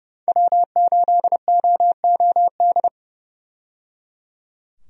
Morse code, heard as a keyed tone, about 30 wpm, 700 Hz. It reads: W8OOD
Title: USA Call Signs - 5 Characters - 30wpm